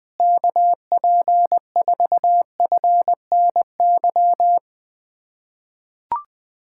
Morse code, heard as a keyed tone, about 20 wpm, 700 Hz. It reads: KP4FNY
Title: USA Callsign DE Callsign 20wpm